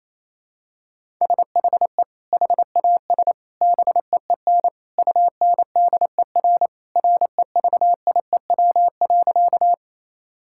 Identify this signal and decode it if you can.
Morse code, heard as a keyed tone, about 28 wpm, 700 Hz. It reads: SHE HAS BEEN UNDER REVIEW.
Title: Sentences from Top 1000 Words - Review Part 6 - 28wpm